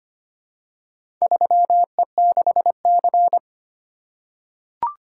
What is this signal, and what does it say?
Morse code, heard as a keyed tone, about 25 wpm, 700 Hz. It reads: 3E6C
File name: practice / 25wpm / Four Letter-Number Combinations - 25wpm